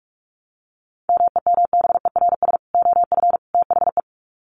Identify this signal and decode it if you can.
Morse code, heard as a keyed tone, about 44 wpm, 700 Hz. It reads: MEMBERS OF THE
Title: Sets of 3 Words 45wpm